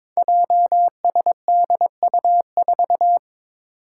Morse code, heard as a keyed tone, about 22 wpm, 700 Hz. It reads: JSDU4
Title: Five Challenging Letter-Number Combinations 22wpm